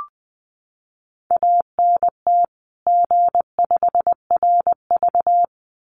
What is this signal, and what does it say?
Morse code, heard as a keyed tone, about 20 wpm, 700 Hz. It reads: ANT G5RV
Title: Common QSO Phrases 20wpm